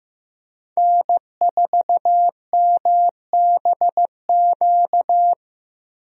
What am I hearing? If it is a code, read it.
Morse code, heard as a keyed tone, about 15 wpm, 700 Hz. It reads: N4MBQ
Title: USA Call Signs - 5 Characters - Repeated 3x - 15wpm